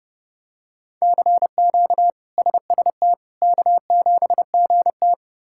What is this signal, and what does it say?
Morse code, heard as a keyed tone, about 32 wpm, 700 Hz. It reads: CQ SST K7GT
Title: K1USN SST QSO Practice - 30wpm